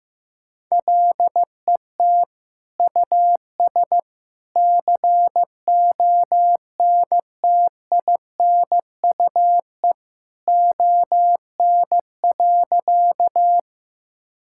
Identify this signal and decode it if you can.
Morse code, heard as a keyed tone, about 15 wpm, 700 Hz. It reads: LET US CONTINUE ON.
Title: Sentences from Top 1000 Words - Review Part 6 - 15wpm